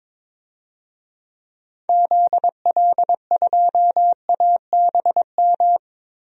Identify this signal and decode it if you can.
Morse code, heard as a keyed tone, about 22 wpm, 700 Hz. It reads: ZL2ABM
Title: DXCC-100 Call Signs - 6 Characters - 22wpm